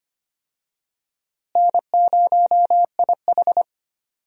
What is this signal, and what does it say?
Morse code, heard as a keyed tone, about 25 wpm, 700 Hz. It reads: N0IH
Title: USA Call Signs - 4 Characters - Repeated 2x - 25wpm